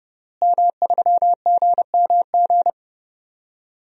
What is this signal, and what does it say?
Morse code, heard as a keyed tone, about 30 wpm, 700 Hz. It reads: M3GMG
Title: DXCC-100 Call Signs - 5 Characters - Repeated 3x - 30wpm